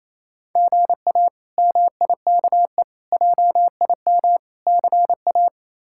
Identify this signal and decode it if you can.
Morse code, heard as a keyed tone, about 28 wpm, 700 Hz. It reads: GA MIKE JIM CA
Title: K1USN SST QSO Practice - 28wpm